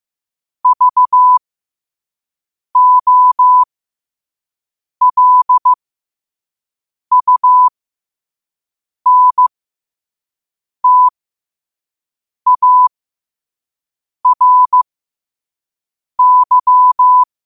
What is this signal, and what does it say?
Morse code, heard as a keyed tone, about 15 wpm, spaced as for 5 wpm, 1000 Hz. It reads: VOLUNTARY
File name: qotdfiles / 5wpm